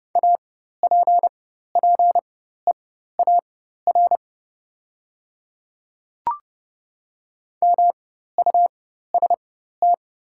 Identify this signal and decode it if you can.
Morse code, heard as a keyed tone, about 30 wpm, spaced as for 14 wpm, 690 Hz. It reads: APPEAR MUST
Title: 4x Character Spacing - Top 500 Words - 30wpm